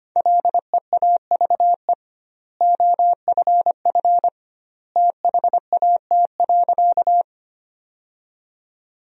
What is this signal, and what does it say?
Morse code, heard as a keyed tone, about 25 wpm, 700 Hz. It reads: LEAVE OFF THAT.
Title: 2x Word-Spacing - Sentences from Top 400 Words - 25wpm